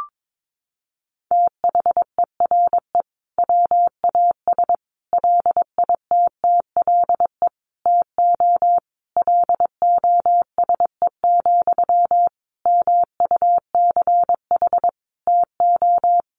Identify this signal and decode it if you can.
Morse code, heard as a keyed tone, about 22 wpm, 700 Hz. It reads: THERE WAS LITTLE TO LOSE, MUCH TO
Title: Sentences from Top 1100 Words 22wpm